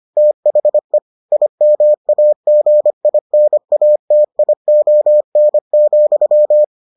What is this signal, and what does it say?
Morse code, heard as a keyed tone, about 25 wpm, 600 Hz. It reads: THE IMAGINATION,